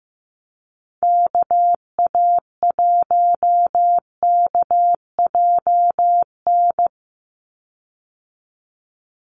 Morse code, heard as a keyed tone, about 15 wpm, 700 Hz. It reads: KA1KJN
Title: USA Call Signs 15wpm